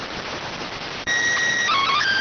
SSTV (slow-scan television), PD 290.